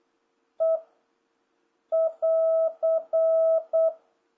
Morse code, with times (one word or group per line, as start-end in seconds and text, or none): 0.6-0.8 E
1.9-3.9 <AR>